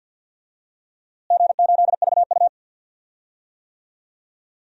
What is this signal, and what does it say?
Morse code, heard as a keyed tone, about 52 wpm, 700 Hz. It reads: G8VU
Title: DXCC-100 Call Signs - 4 Characters - Repeated 2x - 50wpm